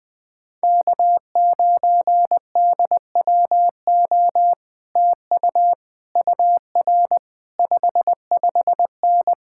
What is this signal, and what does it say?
Morse code, heard as a keyed tone, about 19 wpm, 700 Hz. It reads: K9DWO TU UR 55N
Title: Speed-Racing Common QSO Phrases Encore - 20wpm